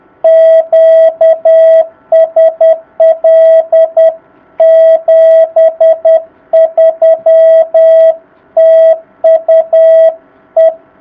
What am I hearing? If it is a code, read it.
Morse code, heard as a keyed tone, about 10 wpm, 660 Hz. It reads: QSL 73 TU E
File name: jh8cbh / 2019cw